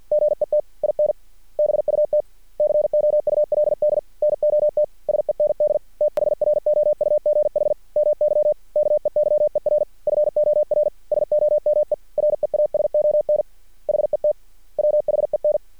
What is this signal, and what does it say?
Morse code, heard as a keyed tone, about 50 wpm, 600 Hz. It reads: GET IN BUT COULD NOT SEND THROUGH MY KEYER FOR SOME REASON HET WHEN
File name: archived / CRSnet